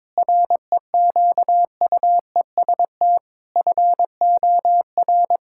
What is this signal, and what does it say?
Morse code, heard as a keyed tone, about 22 wpm, 700 Hz. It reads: REQUEST FOR
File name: practice / 22wpm / Sentences with 8-Word Phrases - 22wpm